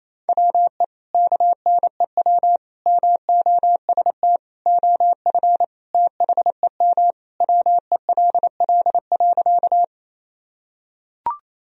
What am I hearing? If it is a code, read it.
Morse code, heard as a keyed tone, about 28 wpm, 700 Hz. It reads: WE KNEW MOST OF THEM WELL.
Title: Sentences from Top 1000 Words - Review Part 1 - 28wpm